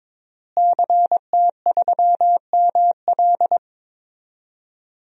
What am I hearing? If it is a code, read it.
Morse code, heard as a keyed tone, about 22 wpm, 700 Hz. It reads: CT3ML
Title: DXCC-100 Call Signs - 5 Characters - Repeated 3x - 22wpm